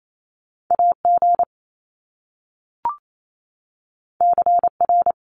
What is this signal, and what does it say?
Morse code, heard as a keyed tone, about 28 wpm, 700 Hz. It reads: AG CR
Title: Top Two Letter Combinations in Words 28wpm